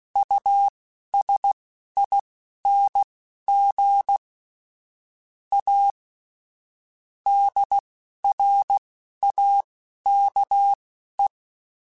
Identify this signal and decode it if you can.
Morse code, heard as a keyed tone, about 16 wpm, spaced as for 11 wpm, 800 Hz. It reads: USING A DRAKE
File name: QSOs / QSOs-mp3s / WD0FBT 1@15